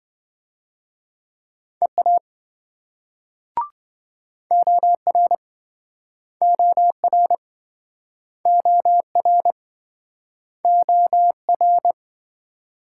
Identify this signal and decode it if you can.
Morse code, heard as a keyed tone, about 26 wpm, 700 Hz. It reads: EA OR OR OR OR